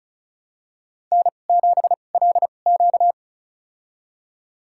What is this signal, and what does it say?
Morse code, heard as a keyed tone, about 36 wpm, spaced as for 25 wpm, 700 Hz. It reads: N7LQ
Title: USA Call Signs - 4 Characters - 2x Character Spacing - 35wpm